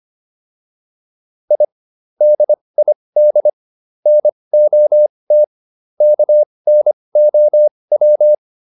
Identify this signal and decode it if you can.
Morse code, heard as a keyed tone, about 25 wpm, spaced as for 20 wpm, 600 Hz. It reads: I DID NOT KNOW